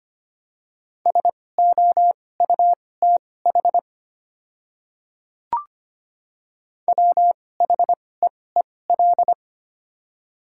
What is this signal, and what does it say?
Morse code, heard as a keyed tone, about 25 wpm, spaced as for 18 wpm, 700 Hz. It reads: SOUTH WHEEL
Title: 2x Character Spacing - Top 500 Words - 25wpm